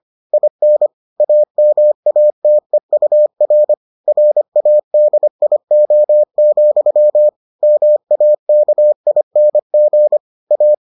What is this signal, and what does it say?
Morse code, heard as a keyed tone, about 25 wpm, 600 Hz. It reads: IN AMATEUR RADIO, MAKING A